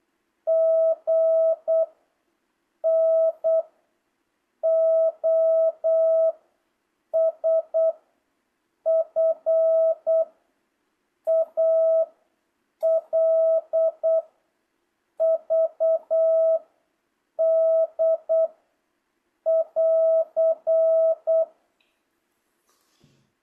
Morse code, heard as a keyed tone, about 8 wpm, 660 Hz. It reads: G N O S F A L V D <AR>